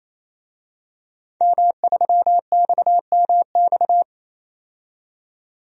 Morse code, heard as a keyed tone, about 28 wpm, 700 Hz. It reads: M3XMX
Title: DXCC-100 Call Signs - 5 Characters - 28wpm